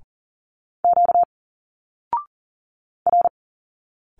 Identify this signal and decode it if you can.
Morse code, heard as a keyed tone, about 40 wpm, 700 Hz. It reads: Q R
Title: Single Letters 40wpm